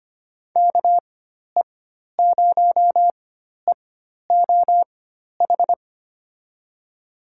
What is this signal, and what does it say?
Morse code, heard as a keyed tone, about 25 wpm, spaced as for 12 wpm, 700 Hz. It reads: KE0EOH
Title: USA Call Signs - 6 Characters - 4x Character Spacing - 25wpm